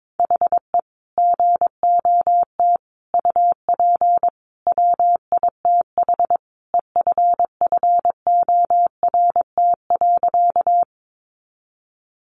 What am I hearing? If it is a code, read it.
Morse code, heard as a keyed tone, about 22 wpm, 700 Hz. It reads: HE GOT UP WITH EFFORT.
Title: Sentences from Top 1200 Words 22wpm